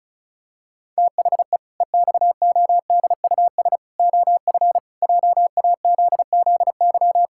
Morse code, heard as a keyed tone, about 36 wpm, 700 Hz. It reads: THE EXODUS OF JAZZY
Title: Pangram Challenge v2 - 35wpm